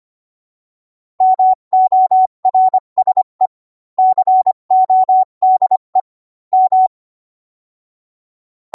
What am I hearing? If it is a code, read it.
Morse code, heard as a keyed tone, about 25 wpm, 750 Hz. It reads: MORSE CODE M